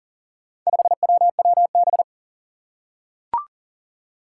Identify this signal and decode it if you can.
Morse code, heard as a keyed tone, about 40 wpm, 690 Hz. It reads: 5WWB
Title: Four Challenging Letter-Number Combinations 40wpm